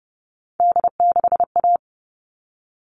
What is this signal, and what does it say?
Morse code, heard as a keyed tone, about 30 wpm, 700 Hz. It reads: D6A
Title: All Cut Numbers - 3-Digit 30wpm